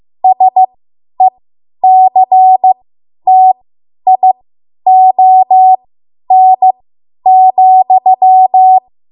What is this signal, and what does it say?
Morse code, heard as a keyed tone, about 15 wpm, spaced as for 10 wpm, 750 Hz. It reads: SECTION,
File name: Morse / 10WPM